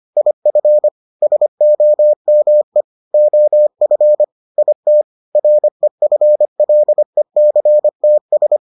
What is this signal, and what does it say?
Morse code, heard as a keyed tone, about 25 wpm, 600 Hz. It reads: IF SOME OF IT REFLECTS